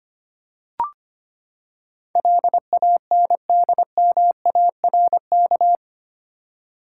Morse code, heard as a keyed tone, about 25 wpm, 700 Hz. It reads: LANDMARK
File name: Eight Letter Words - 25wpm